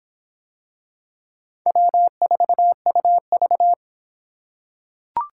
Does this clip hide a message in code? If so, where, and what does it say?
Yes: Morse code from 1.7 to 3.7 s, heard as a keyed tone, about 26 wpm, 700 Hz. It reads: W4UV